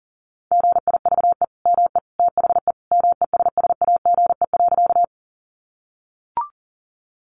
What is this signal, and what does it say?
Morse code, heard as a keyed tone, about 40 wpm, 700 Hz. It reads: GIVE ME THE MESSAGE.
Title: Sentences from Top 1100 Words 40wpm